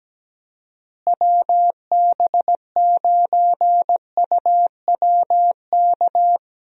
Morse code, heard as a keyed tone, about 17 wpm, 700 Hz. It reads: WB9UWK
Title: USA Call Signs - 6 Characters - Repeated 2x - 17wpm